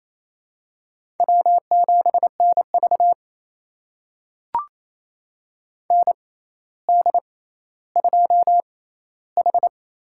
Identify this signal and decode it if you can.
Morse code, heard as a keyed tone, about 28 wpm, spaced as for 14 wpm, 700 Hz. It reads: W7NV ND2H
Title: USA Call Signs - 4 Characters - 6x Character Spacing - 28wpm